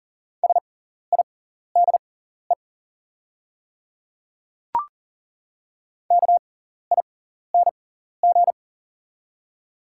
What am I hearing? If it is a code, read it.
Morse code, heard as a keyed tone, about 40 wpm, spaced as for 14 wpm, 700 Hz. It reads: SIDE KING
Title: 6x Character Spacing - Top 500 Words - 40wpm